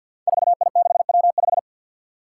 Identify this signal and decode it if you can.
Morse code, heard as a keyed tone, about 52 wpm, 700 Hz. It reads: 4I6W5
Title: Five Challenging Letter-Number Combinations 50wpm